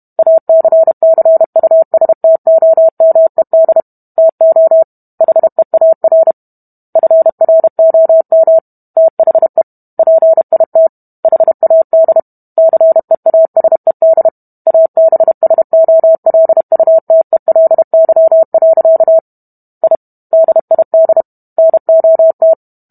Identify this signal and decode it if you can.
Morse code, heard as a keyed tone, about 32 wpm, 650 Hz. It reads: ACCUSTOMED TO HEAR FROM THE PIT HAD CEASED ABSOLUTELY. I DID NOT